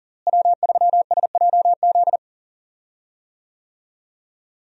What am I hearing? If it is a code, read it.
Morse code, heard as a keyed tone, about 40 wpm, 700 Hz. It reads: W3SJZ
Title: USA Call Signs - 5 Characters - Repeated 2x - 40wpm